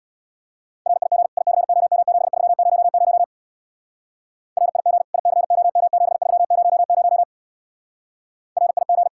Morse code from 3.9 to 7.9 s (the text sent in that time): RIG ICOM7300